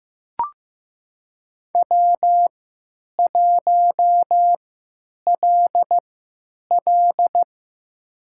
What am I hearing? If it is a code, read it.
Morse code, heard as a keyed tone, about 15 wpm, spaced as for 8 wpm, 700 Hz. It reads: W1LL
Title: USA Call Signs - 4 Characters - 3x Character Spacing - 15wpm